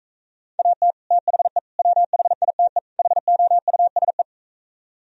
Morse code, heard as a keyed tone, about 45 wpm, 700 Hz. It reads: AT THE WHITE HOUSE